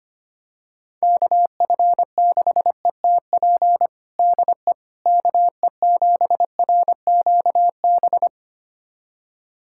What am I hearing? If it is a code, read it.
Morse code, heard as a keyed tone, about 25 wpm, 700 Hz. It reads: KF6ETP DE KE7RQB
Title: USA Callsign DE Callsign - 6 Characters - 25wpm